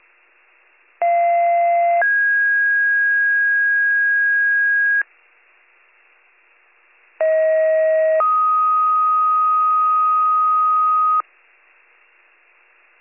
Two-tone page, heard as a one-second tone, then a three-second tone, about 690 then 1650 Hz, about 640 then 1250 Hz. It is heard from 1.0 to 11.2 s.